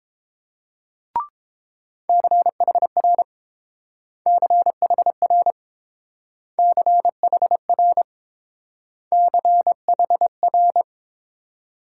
Morse code, heard as a keyed tone, about 27 wpm, 700 Hz. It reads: CHR CHR CHR CHR
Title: Speed-Racing Multi-letter Phonemes - 22wpm